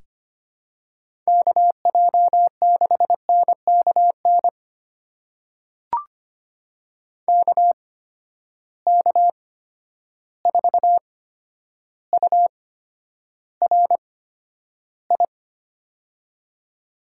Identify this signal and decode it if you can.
Morse code, heard as a keyed tone, about 25 wpm, spaced as for 11 wpm, 700 Hz. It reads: KJ6NKN KK4URI
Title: USA Call Signs - 6 Characters - 8x Character Spacing - 25wpm